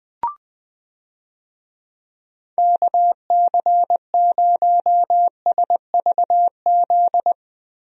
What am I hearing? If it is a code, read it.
Morse code, heard as a keyed tone, about 20 wpm, 700 Hz. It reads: KC0SVZ
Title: USA Call Signs - 6 Characters - Repeated 3x - 20wpm